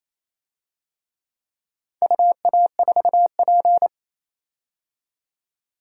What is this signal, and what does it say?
Morse code, heard as a keyed tone, about 27 wpm, 700 Hz. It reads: UA4P
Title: DXCC-100 Call Signs - 4 Characters - 28wpm